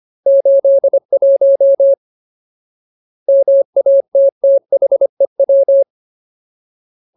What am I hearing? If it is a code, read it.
Morse code, heard as a keyed tone, about 25 wpm, 550 Hz. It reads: 81 MATTHEW
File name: Ultimate Word List / 25WPM Ultimate